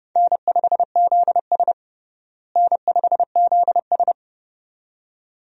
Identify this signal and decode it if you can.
Morse code, heard as a keyed tone, about 32 wpm, 700 Hz. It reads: N5ZS N5ZS
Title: USA Call Signs - 4 Characters - Repeated 2x - 30wpm